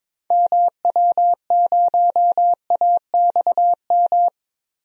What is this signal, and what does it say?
Morse code, heard as a keyed tone, about 22 wpm, 700 Hz. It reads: MW0AXM
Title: DXCC-100 Call Signs - 6 Characters - Repeated 2x - 22wpm